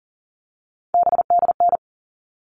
Morse code, heard as a keyed tone, about 40 wpm, 700 Hz. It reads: BDN
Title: All Cut Numbers - 3-Digit 40wpm